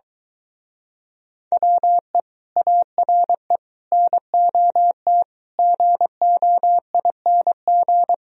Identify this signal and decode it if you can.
Morse code, heard as a keyed tone, about 22 wpm, 700 Hz. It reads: WE ARE NOT GOING